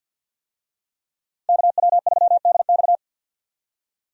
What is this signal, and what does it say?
Morse code, heard as a keyed tone, about 47 wpm, 700 Hz. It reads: KW3DX
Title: USA Call Signs - 5 Characters - Repeated 3x - 50wpm